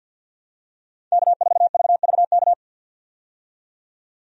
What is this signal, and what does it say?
Morse code, heard as a keyed tone, about 48 wpm, 700 Hz. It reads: K4VVK
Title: USA Call Signs - 5 Characters - 50wpm